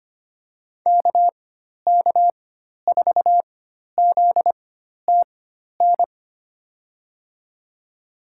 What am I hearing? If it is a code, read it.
Morse code, heard as a keyed tone, about 25 wpm, spaced as for 12 wpm, 700 Hz. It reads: KK4ZTN
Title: USA Call Signs - 6 Characters - 4x Character Spacing - 25wpm